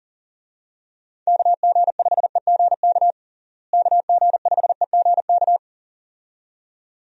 Morse code, heard as a keyed tone, about 40 wpm, 700 Hz. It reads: KG5EGK KG5EGK
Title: USA Call Signs - 6 Characters - Repeated 2x - 40wpm